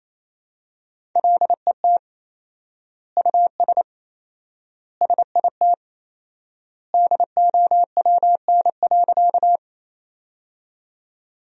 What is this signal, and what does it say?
Morse code, heard as a keyed tone, about 28 wpm, 700 Hz. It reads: LET US SIT DOWN.
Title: 4x Word-Spacing - Sentences from Top 500 Words - 28wpm